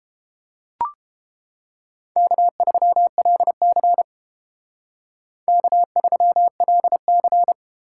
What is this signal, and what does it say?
Morse code, heard as a keyed tone, about 30 wpm, 700 Hz. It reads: K3LC K3LC